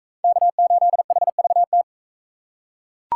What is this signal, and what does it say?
Morse code, heard as a keyed tone, about 38 wpm, 700 Hz. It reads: K8HVT